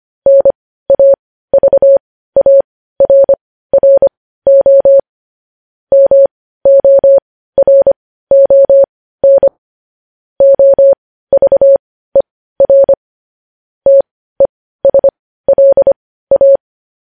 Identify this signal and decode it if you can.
Morse code, heard as a keyed tone, about 25 wpm, spaced as for 15 wpm, 550 Hz. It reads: NAVARRO MORON OVER TESLA
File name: podcast / News headlines-1x-25x15